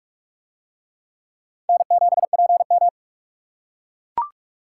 Morse code, heard as a keyed tone, about 42 wpm, 700 Hz. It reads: N7PM